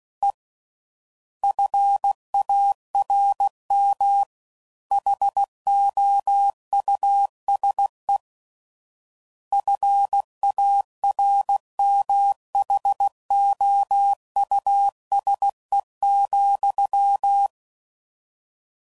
Morse code, heard as a keyed tone, about 16 wpm, 800 Hz. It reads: E FARM HOUSE FARMHOUSE,
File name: Suffixes / General / Houses